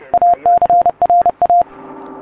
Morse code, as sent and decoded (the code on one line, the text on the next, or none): .- -.-. .-. .-
ACRA